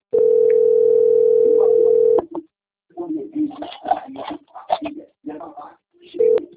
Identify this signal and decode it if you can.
Ringback tone (North America), 440 + 480 Hz, with speech.
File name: out- - - - - .13643